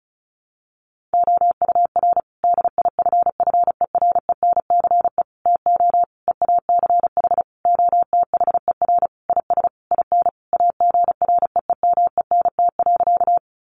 Morse code, heard as a keyed tone, about 34 wpm, 700 Hz. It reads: OUR DIFFERENCE TO EACH OTHER IS IN AGREEMENT.
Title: Sentences from Top 1500 Words 35wpm